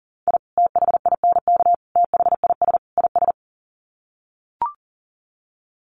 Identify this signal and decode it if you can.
Morse code, heard as a keyed tone, about 40 wpm, 700 Hz. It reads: I THINK THIS IS
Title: Sets of 4 Words 40wpm